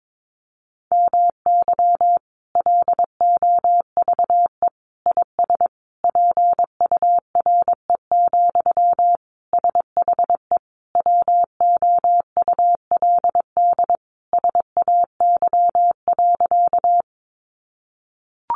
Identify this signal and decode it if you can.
Morse code, heard as a keyed tone, about 22 wpm, 700 Hz. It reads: MY LOVE IS PURE, SHE WOULD SAY.
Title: Sentences from Top 1500 Words 22wpm